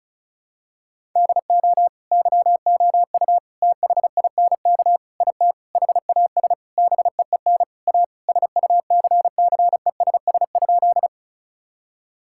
Morse code, heard as a keyed tone, about 36 wpm, 700 Hz. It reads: DO YOU THINK IT HAS BEEN A SUCCESS?
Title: Sentences from Top 1000 Words - Review Part 6 - 35wpm